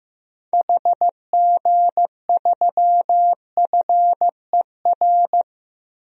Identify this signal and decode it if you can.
Morse code, heard as a keyed tone, about 16 wpm, 700 Hz. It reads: HG3FER
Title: DXCC-100 Call Signs - 6 Characters - Repeated 2x - 15wpm